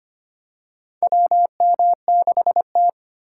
Morse code, heard as a keyed tone, about 25 wpm, 700 Hz. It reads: WM6T
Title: USA Call Signs - 4 Characters - 2x Character Spacing - 25wpm